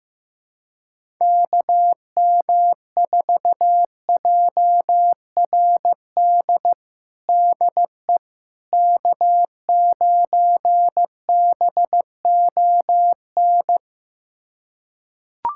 Morse code, heard as a keyed tone, about 15 wpm, 700 Hz. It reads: KM4JRD DE K9BON
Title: USA Callsign DE Callsign 15wpm